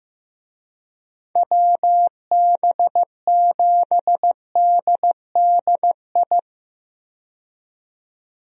Morse code, heard as a keyed tone, about 15 wpm, 700 Hz. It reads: WB7DDI
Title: USA Call Signs - 6 Characters - 15wpm